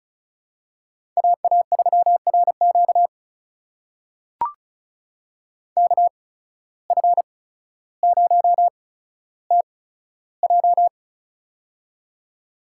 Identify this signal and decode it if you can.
Morse code, heard as a keyed tone, about 36 wpm, spaced as for 15 wpm, 700 Hz. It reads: AA3RQ KF0TJ
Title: USA Call Signs - 5 Characters - 8x Character Spacing - 35wpm